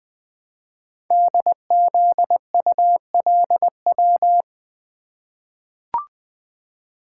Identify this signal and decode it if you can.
Morse code, heard as a keyed tone, about 20 wpm, 700 Hz. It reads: DZULW
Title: Code Groups - 5 Letters - 20wpm